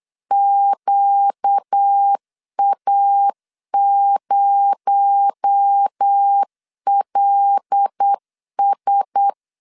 Morse code, heard as a keyed tone, about 8 wpm, 800 Hz. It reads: QA0LS
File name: LEZIONI / 6 PKX0 / 6 PKX0 8wpm 3341